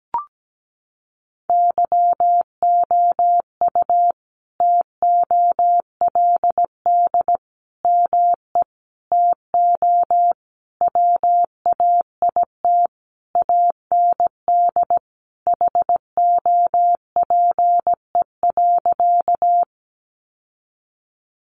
Morse code, heard as a keyed tone, about 17 wpm, 700 Hz. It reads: YOU TOLD ME TO WAIT AND HOPE.
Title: Sentences from Top 1000 Words 17wpm